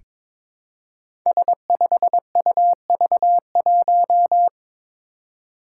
Morse code, heard as a keyed tone, about 22 wpm, 700 Hz. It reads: S5UV1
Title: Five Challenging Letter-Number Combinations 22wpm